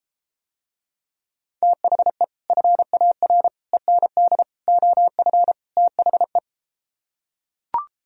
Morse code, heard as a keyed tone, about 33 wpm, 700 Hz. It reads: THE FAR END OF THE